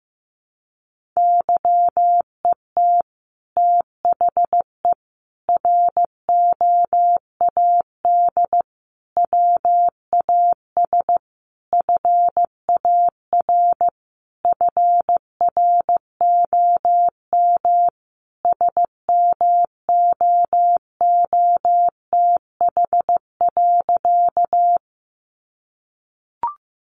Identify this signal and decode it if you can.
Morse code, heard as a keyed tone, about 15 wpm, 700 Hz. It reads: YET THE ROAD WAS FAR FROM SMOOTH.
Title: Sentences from Top 1400 Words 15wpm